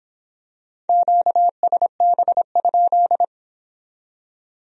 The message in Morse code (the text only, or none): QSB?